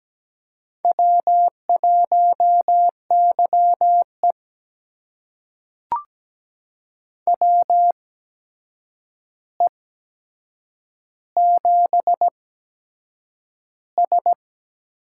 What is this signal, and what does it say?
Morse code, heard as a keyed tone, about 17 wpm, spaced as for 7 wpm, 700 Hz. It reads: W1YE WE7S
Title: USA Call Signs - 4 Characters - 8x Character Spacing - 17wpm